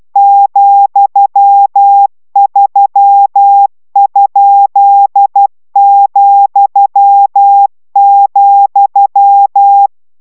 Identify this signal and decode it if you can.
Morse code, heard as a keyed tone, about 12 wpm, 800 Hz. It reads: ,3?,,